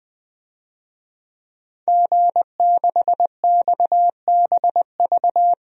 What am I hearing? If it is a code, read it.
Morse code, heard as a keyed tone, about 20 wpm, 700 Hz. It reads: G6XBV
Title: DXCC-100 Call Signs - 5 Characters - 20wpm